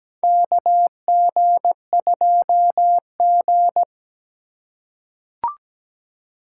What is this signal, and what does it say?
Morse code, heard as a keyed tone, about 17 wpm, 700 Hz. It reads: KG2G
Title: USA Call Signs - 4 Characters - 3x Character Spacing - 17wpm